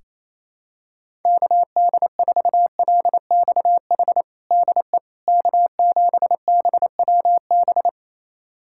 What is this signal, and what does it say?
Morse code, heard as a keyed tone, about 28 wpm, 700 Hz. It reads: KD4LXH DE K7BWB
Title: USA Callsign DE Callsign 28wpm